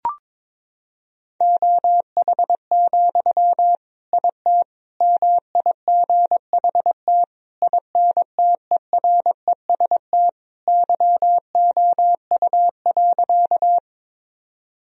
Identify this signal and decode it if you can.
Morse code, heard as a keyed tone, about 22 wpm, 700 Hz. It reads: OH, IT MIGHT INTEREST YOU.
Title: Sentences from Top 1000 Words - Review Part 2 - 22wpm